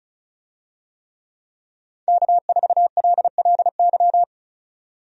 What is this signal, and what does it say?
Morse code, heard as a keyed tone, about 36 wpm, 700 Hz. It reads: K4LLY
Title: USA Call Signs - 5 Characters - 35wpm